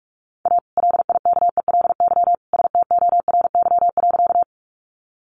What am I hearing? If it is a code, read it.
Morse code, heard as a keyed tone, about 44 wpm, 700 Hz. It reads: A LIKELY STORY.
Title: Sentences from Top 600 Words 45wpm